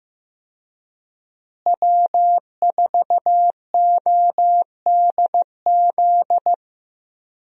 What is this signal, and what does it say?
Morse code, heard as a keyed tone, about 15 wpm, 700 Hz. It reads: W4ODZ
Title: USA Call Signs - 5 Characters - Repeated 3x - 15wpm